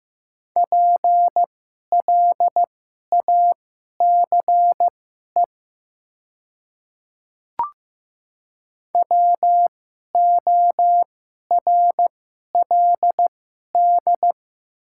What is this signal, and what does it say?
Morse code, heard as a keyed tone, about 15 wpm, spaced as for 11 wpm, 690 Hz. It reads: PLACE WORLD
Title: 2x Character Spacing - Top 200 Words - 15wpm